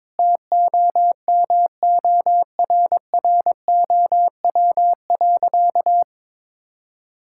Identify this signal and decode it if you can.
Morse code, heard as a keyed tone, about 22 wpm, 700 Hz. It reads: TOMORROW.
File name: practice / 22wpm / Sentences with 3-Word Phrases - 22wpm